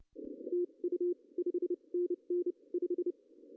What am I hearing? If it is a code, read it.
Morse code, heard as a keyed tone, about 30 wpm, spaced as for 25 wpm, 350 Hz. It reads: TU 5NN 5
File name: CW / cqww-cw-k5tr